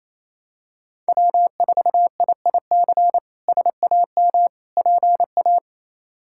Morse code, heard as a keyed tone, about 27 wpm, 700 Hz. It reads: W4IIC SAM PA
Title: K1USN SST QSO Practice - 28wpm